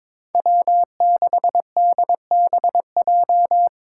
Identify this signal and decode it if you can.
Morse code, heard as a keyed tone, about 25 wpm, 700 Hz. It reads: W6DBJ
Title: Speed Racing - USA 1x3 Call Signs 22wpm